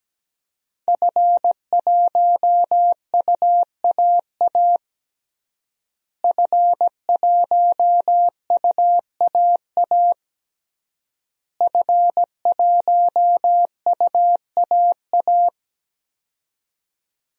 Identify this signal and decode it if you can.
Morse code, heard as a keyed tone, about 17 wpm, 700 Hz. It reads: F1UAA F1UAA F1UAA
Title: DXCC-100 Call Signs - 5 Characters - Repeated 3x - 17wpm